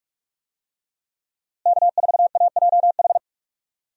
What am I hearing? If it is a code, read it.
Morse code, heard as a keyed tone, about 41 wpm, 700 Hz. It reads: K4AJH
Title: USA Call Signs - 5 Characters - Repeated 2x - 45wpm